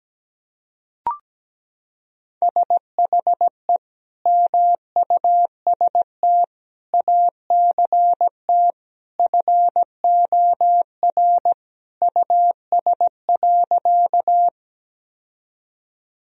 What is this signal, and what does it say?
Morse code, heard as a keyed tone, about 17 wpm, 700 Hz. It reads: SHE MUST ACT FOR US.
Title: Sentences from Top 1000 Words - Review Part 1 - 17wpm